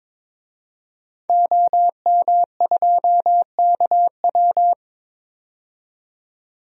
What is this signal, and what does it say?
Morse code, heard as a keyed tone, about 22 wpm, 700 Hz. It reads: OM2KW
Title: DXCC-100 Call Signs - 5 Characters - 22wpm